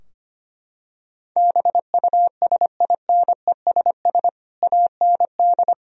Morse code, heard as a keyed tone, about 25 wpm, 700 Hz. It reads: BUSINESS AND
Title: Sentences from Top 1000 Words - Review Part 3 - 25wpm